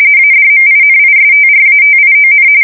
RTTY, about 45.45 baud. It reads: ADMINISTRATION
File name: fsk170 / usnavy3